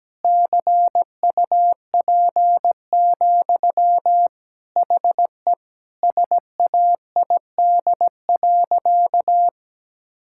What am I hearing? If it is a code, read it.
Morse code, heard as a keyed tone, about 17 wpm, 700 Hz. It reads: CUP, HE SAID.
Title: Sentences from Top 1600 Words 17wpm